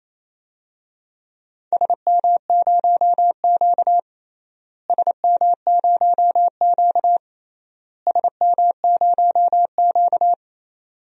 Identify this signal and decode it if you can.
Morse code, heard as a keyed tone, about 28 wpm, 700 Hz. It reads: SM0Q SM0Q SM0Q
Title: DXCC-100 Call Signs - 4 Characters - Repeated 3x - 28wpm